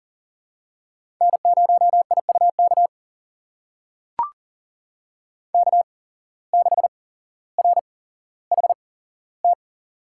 Morse code, heard as a keyed tone, about 40 wpm, spaced as for 17 wpm, 700 Hz. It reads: N0IUK K6RHT
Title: USA Call Signs - 5 Characters - 8x Character Spacing - 40wpm